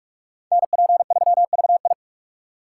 Morse code, heard as a keyed tone, about 56 wpm, 690 Hz. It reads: NP3VI
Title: Speed Racing - USA 2x2 Call Signs 45wpm